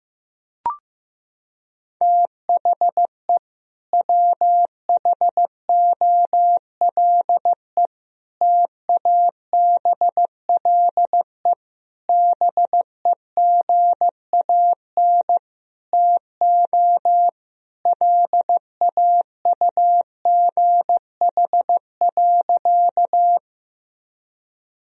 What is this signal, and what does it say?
Morse code, heard as a keyed tone, about 14 wpm, 700 Hz. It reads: THE WHOLE TABLE BEGAN TO LAUGH.
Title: Sentences from Top 1000 Words - Review Part 2 - 15wpm